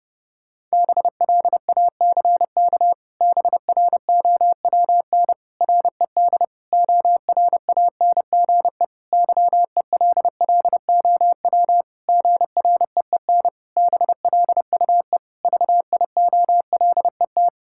Morse code, heard as a keyed tone, about 30 wpm, 700 Hz. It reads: BLACK BROWN RED ORANGE YELLOW GREEN BLUE VIOLET
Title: Pangram Challenge v2 - 30wpm